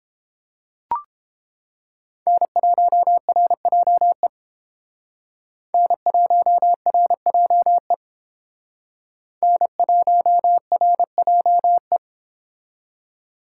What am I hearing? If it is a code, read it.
Morse code, heard as a keyed tone, about 29 wpm, 700 Hz. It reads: N1RJE N1RJE N1RJE